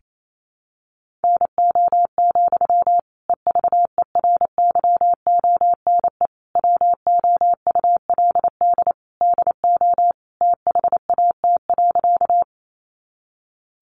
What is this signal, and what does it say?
Morse code, heard as a keyed tone, about 28 wpm, 700 Hz. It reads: NO, EVERYONE WOULD DO THAT.
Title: Sentences from Top 800 Words 28wpm